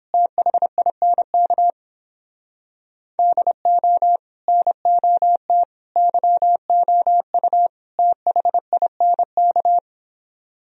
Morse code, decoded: THINK DO NOT YOU THINK

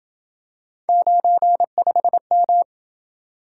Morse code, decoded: 95M

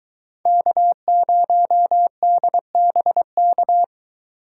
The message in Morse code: K0DBK

